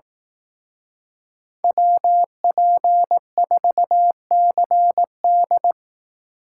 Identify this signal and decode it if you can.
Morse code, heard as a keyed tone, about 18 wpm, 700 Hz. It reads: WP4CD
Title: Speed Racing - USA 2x2 Call Signs 15wpm